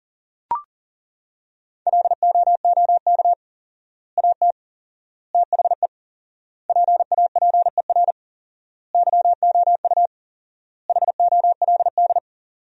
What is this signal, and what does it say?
Morse code, heard as a keyed tone, about 37 wpm, 700 Hz. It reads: LOOK AT THE PAPER YOU HOLD
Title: 4x Word-Spacing - Sentences from Top 400 Words - 40wpm